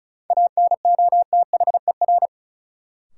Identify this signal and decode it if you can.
Morse code, heard as a keyed tone, about 36 wpm, 700 Hz. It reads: ANOTHER
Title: Sentences from Top 600 Words - New Word Drill - 35wpm